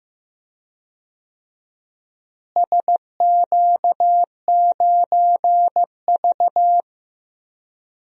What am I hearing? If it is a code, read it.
Morse code, heard as a keyed tone, about 15 wpm, 700 Hz. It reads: SQ9V
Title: DXCC-100 Call Signs - 4 Characters - Repeated 2x - 15wpm